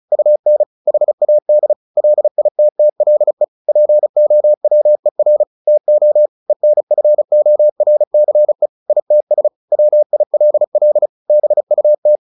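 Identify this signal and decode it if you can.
Morse code, heard as a keyed tone, about 34 wpm, 600 Hz. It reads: UN HAD LITTLE POWER TO ENFORCE ITS WILL BUT
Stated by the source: Morse practice at 35 wpm from CW med Gnister, sending Ken Follett's novel Never in English.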